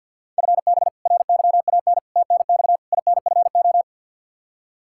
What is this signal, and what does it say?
Morse code, heard as a keyed tone, about 52 wpm, 700 Hz. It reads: FB RYAN TNX INFO
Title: K1USN SST QSO Practice 50wpm